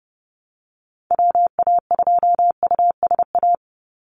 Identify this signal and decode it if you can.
Morse code, heard as a keyed tone, about 30 wpm, 700 Hz. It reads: WA2USA
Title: NA SOTA Activator Callsigns 30wpm